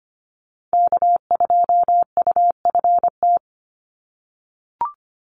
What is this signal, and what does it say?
Morse code, heard as a keyed tone, about 25 wpm, 700 Hz. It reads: K2UFT
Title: Call Signs - 25wpm